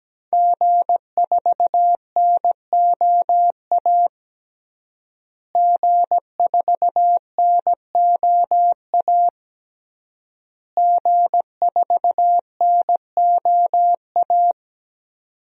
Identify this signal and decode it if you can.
Morse code, heard as a keyed tone, about 17 wpm, 700 Hz. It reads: G4NOA G4NOA G4NOA
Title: DXCC-100 Call Signs - 5 Characters - Repeated 3x - 17wpm